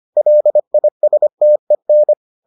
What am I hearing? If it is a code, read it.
Morse code, heard as a keyed tone, about 25 wpm, 600 Hz. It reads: LISTEN